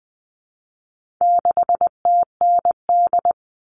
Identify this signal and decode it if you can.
Morse code, heard as a keyed tone, about 20 wpm, 700 Hz. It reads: 6TND
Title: All Cut Numbers - 4-Digit 20wpm